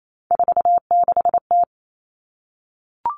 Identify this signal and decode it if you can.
Morse code, heard as a keyed tone, about 28 wpm, 700 Hz. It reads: 46T
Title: Common Cut Numbers - 3-Digit 28wpm